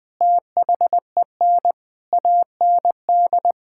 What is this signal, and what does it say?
Morse code, heard as a keyed tone, about 21 wpm, 700 Hz. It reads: THEN AND
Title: Sentences with 8-Word Phrases - Spoken Beginning - 20wpm